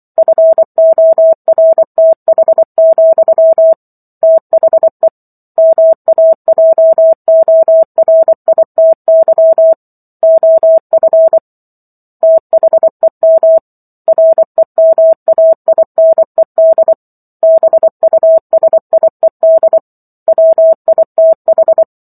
Morse code, heard as a keyed tone, about 24 wpm, 650 Hz. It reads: FORTH, THE MAJORITY OF THEM REMAINED BUSIED WITH